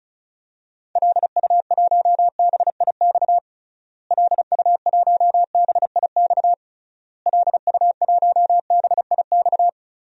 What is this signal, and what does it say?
Morse code, heard as a keyed tone, about 36 wpm, 700 Hz. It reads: LU1BIX LU1BIX LU1BIX
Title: DXCC-100 Call Signs - 6 Characters - Repeated 3x - 35wpm